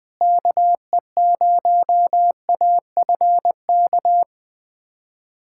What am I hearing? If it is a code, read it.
Morse code, heard as a keyed tone, about 20 wpm, 700 Hz. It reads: KE0AFK
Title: USA Callsign DE Callsign 20wpm